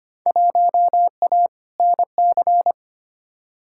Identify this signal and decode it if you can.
Morse code, heard as a keyed tone, about 25 wpm, 700 Hz. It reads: 1A NC
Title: Field Day Exchanges Encore - 25wpm